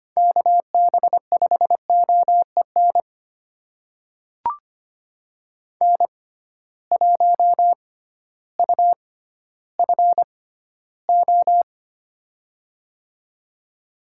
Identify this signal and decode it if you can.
Morse code, heard as a keyed tone, about 25 wpm, spaced as for 13 wpm, 700 Hz. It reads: KB5OEN N1UFO
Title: USA Call Signs - 6x Character Spacing - 25wpm